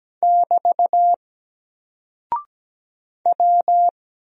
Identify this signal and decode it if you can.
Morse code, heard as a keyed tone, about 17 wpm, 700 Hz. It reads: <BT> W
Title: Single Character (Letter:Number:Punctuation:Prosign) Alt 17wpm